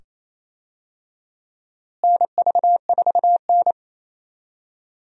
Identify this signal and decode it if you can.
Morse code, heard as a keyed tone, about 28 wpm, 700 Hz. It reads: NV4N
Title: USA Call Signs - 4 Characters - Repeated 2x - 28wpm